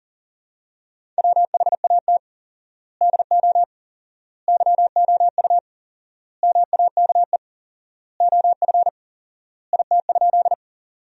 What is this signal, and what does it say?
Morse code, heard as a keyed tone, about 40 wpm, 700 Hz. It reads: WHAT DO YOU MAKE OF IT?
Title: 4x Word-Spacing - Sentences from Top 100 Words - 40wpm